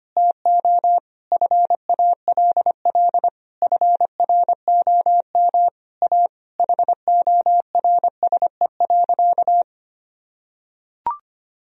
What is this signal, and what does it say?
Morse code, heard as a keyed tone, about 25 wpm, 700 Hz. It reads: TO FALL FROM A HORSE.
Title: Sentences from Top 1000 Words - Review Part 2 - 25wpm